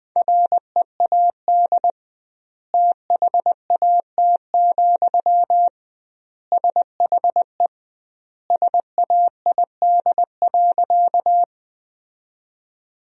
Morse code, heard as a keyed tone, about 20 wpm, 700 Hz. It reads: READ THAT, SHE SAID.
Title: 2x Word-Spacing - Sentences from Top 200 Words - 20wpm